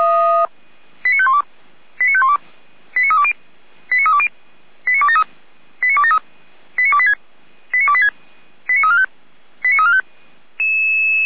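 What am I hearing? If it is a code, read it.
Five-tone selective call (ZVEI), heard as quick runs of five tones, tones of about 70 ms. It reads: 89512 89512 89320 89320 89273 89273 89286 89286 89345 89345
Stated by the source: ZVEI five-tone selective call received by SDR.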